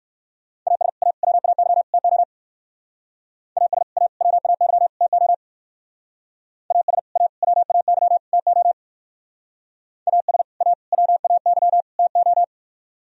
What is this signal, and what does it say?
Morse code, heard as a keyed tone, about 56 wpm, 700 Hz. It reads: AS A WAY TO AS A WAY TO AS A WAY TO AS A WAY TO